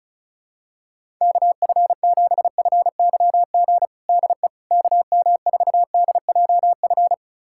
Morse code, heard as a keyed tone, about 36 wpm, 700 Hz. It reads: KF7FYG DE KM4DJF
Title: USA Callsign DE Callsign 35wpm